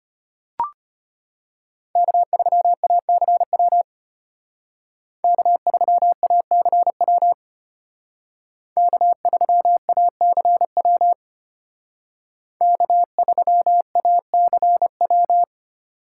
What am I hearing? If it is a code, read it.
Morse code, heard as a keyed tone, about 33 wpm, 700 Hz. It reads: K3ACW K3ACW K3ACW K3ACW